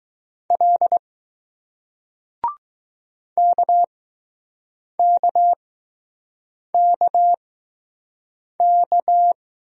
Morse code, heard as a keyed tone, about 20 wpm, spaced as for 6 wpm, 700 Hz. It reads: L KKKK